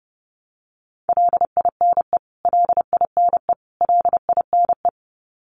Morse code, heard as a keyed tone, about 30 wpm, 700 Hz. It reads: LINE LINE LINE
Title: Top 300 Words Repeated 30wpm